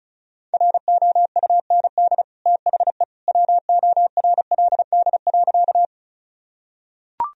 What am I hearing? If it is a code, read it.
Morse code, heard as a keyed tone, about 36 wpm, 700 Hz. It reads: ROUND THE WORLD.
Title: Sentences from Top 1000 Words - Review Part 1 - 35wpm